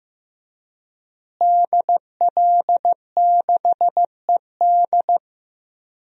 Morse code, heard as a keyed tone, about 15 wpm, 700 Hz. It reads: DL6ED
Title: DXCC-100 Call Signs - 5 Characters - Repeated 2x - 15wpm